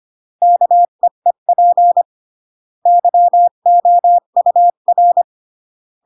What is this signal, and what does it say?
Morse code, heard as a keyed tone, about 25 wpm, 700 Hz. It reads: KEEP YOUR